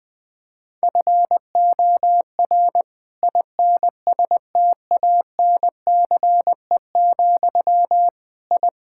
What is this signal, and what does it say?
Morse code, heard as a keyed tone, about 21 wpm, 700 Hz. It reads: FOR INSTANCE, I
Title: Sentences from Top 1000 Words - Review Part 5 - 20wpm